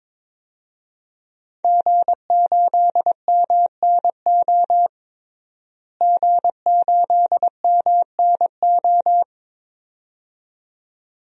Morse code, heard as a keyed tone, about 21 wpm, 700 Hz. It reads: G8MNO G8MNO
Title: DXCC-100 Call Signs - 5 Characters - Repeated 2x - 22wpm